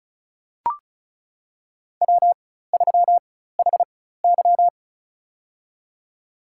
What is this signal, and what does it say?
Morse code, heard as a keyed tone, about 36 wpm, spaced as for 16 wpm, 700 Hz. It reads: W3HY
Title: USA Call Signs - 4 Characters - 4x Character Spacing - 35wpm